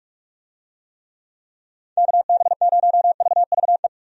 Morse code, heard as a keyed tone, about 46 wpm, 700 Hz. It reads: KB0VVE
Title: USA Call Signs - 6 Characters - 45wpm